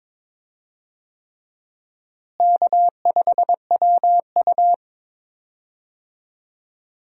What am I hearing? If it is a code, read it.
Morse code, heard as a keyed tone, about 22 wpm, 700 Hz. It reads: K5WU
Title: USA Call Signs - 4 Characters - Repeated 3x - 22wpm